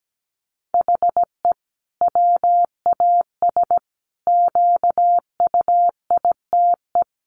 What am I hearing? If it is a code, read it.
Morse code, heard as a keyed tone, about 17 wpm, 700 Hz. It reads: HE WAS QUITE
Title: Sentences from Top 1500 Words 17wpm